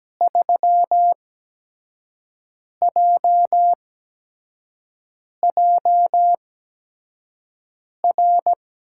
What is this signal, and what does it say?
Morse code, heard as a keyed tone, about 17 wpm, spaced as for 5 wpm, 700 Hz. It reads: 3JJR
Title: USA Call Signs - 6 Characters - 8x Character Spacing - 17wpm